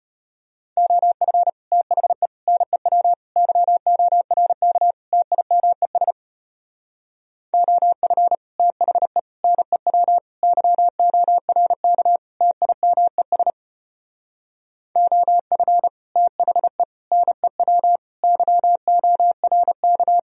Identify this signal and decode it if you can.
Morse code, heard as a keyed tone, about 31 wpm, 700 Hz. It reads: OF THE NEW YORK TIMES OF THE NEW YORK TIMES OF THE NEW YORK
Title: Speed-Racing - Sets of 5 Words 25wpm